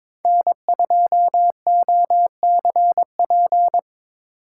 Morse code, heard as a keyed tone, about 25 wpm, 700 Hz. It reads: N2OCP
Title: Speed Racing - USA Call Signs 22wpm